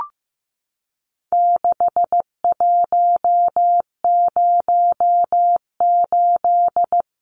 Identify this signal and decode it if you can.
Morse code, heard as a keyed tone, about 15 wpm, 700 Hz. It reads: 6108
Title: Numbers - 4-Digit 15wpm